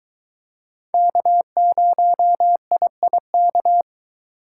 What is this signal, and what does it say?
Morse code, heard as a keyed tone, about 23 wpm, 700 Hz. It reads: K0IIK